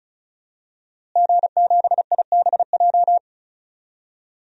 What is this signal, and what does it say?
Morse code, heard as a keyed tone, about 36 wpm, 700 Hz. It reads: G7IBJ
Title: DXCC-100 Call Signs - 5 Characters - Repeated 3x - 35wpm